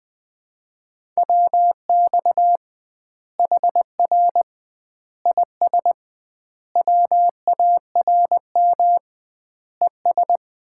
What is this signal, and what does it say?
Morse code, heard as a keyed tone, about 21 wpm, 700 Hz. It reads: WX HR IS WARM ES
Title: Common QSO Phrases - 2x Word-Spacing - 20wpm